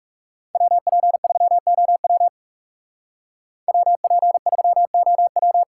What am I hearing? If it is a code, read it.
Morse code, heard as a keyed tone, about 39 wpm, 700 Hz. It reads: WP3OW WP3OW